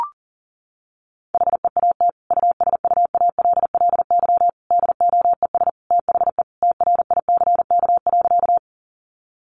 Morse code, heard as a keyed tone, about 40 wpm, 700 Hz. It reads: HEAT USUALLY DOES THE TRICK.
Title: Sentences from Top 1500 Words 40wpm